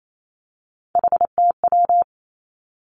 Morse code, heard as a keyed tone, about 28 wpm, 700 Hz. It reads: HTW
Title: Three Challenging Letter-Number Combinations 28wpm